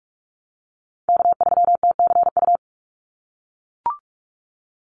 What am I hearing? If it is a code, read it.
Morse code, heard as a keyed tone, about 44 wpm, 700 Hz. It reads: K3TCU
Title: NA SOTA Chaser Callsigns 45wpm